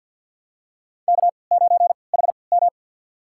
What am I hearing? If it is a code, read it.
Morse code, heard as a keyed tone, about 50 wpm, spaced as for 28 wpm, 700 Hz. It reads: K9HM
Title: USA Call Signs - 3x Character Spacing - 50wpm